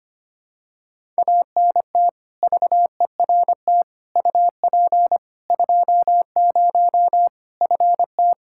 Morse code, heard as a keyed tone, about 25 wpm, 700 Hz. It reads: ANT VERT UP 20 FT
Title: K1USN SST QSO Practice 25wpm